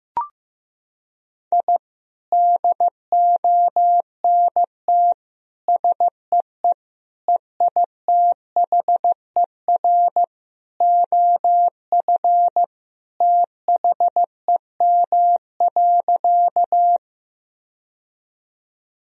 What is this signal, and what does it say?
Morse code, heard as a keyed tone, about 15 wpm, 700 Hz. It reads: I DONT SEE EITHER OF THEM.
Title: Sentences from Top 1000 Words - Review Part 4 - 15wpm